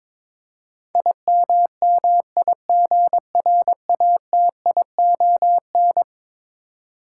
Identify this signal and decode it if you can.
Morse code, heard as a keyed tone, about 22 wpm, 700 Hz. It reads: IMMIGRATION
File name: practice / 22wpm / Eleven Letter Words - 22wpm